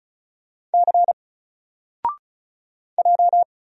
Morse code, heard as a keyed tone, about 34 wpm, 700 Hz. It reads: C J
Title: Single Letter-Number 35wpm